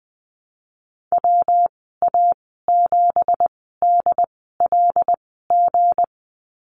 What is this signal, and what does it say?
Morse code, heard as a keyed tone, about 20 wpm, spaced as for 15 wpm, 700 Hz. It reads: WA7DLG
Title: 186 Lesson 30g - Letter b - Call Signs 20wpm